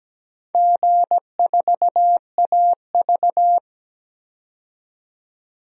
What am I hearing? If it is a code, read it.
Morse code, heard as a keyed tone, about 17 wpm, 700 Hz. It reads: G4AV
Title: DXCC-100 Call Signs - 4 Characters - 17wpm